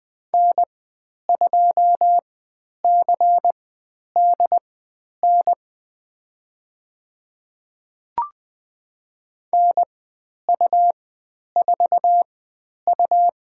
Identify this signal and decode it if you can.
Morse code, heard as a keyed tone, about 20 wpm, spaced as for 10 wpm, 700 Hz. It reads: N2CDN NU4U
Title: 102 Lesson 17g - Letter u - Call Signs 20wpm